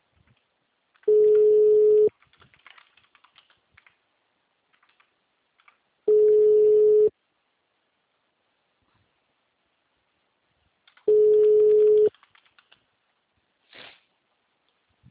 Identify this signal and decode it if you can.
Ringback tone (Europe, 425 Hz).
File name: out- -2016- - - .5561